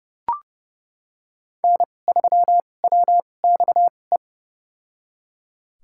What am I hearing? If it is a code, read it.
Morse code, heard as a keyed tone, about 30 wpm, spaced as for 22 wpm, 700 Hz. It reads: N3WXE
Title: USA Call Signs - 2x Character Spacing - 30wpm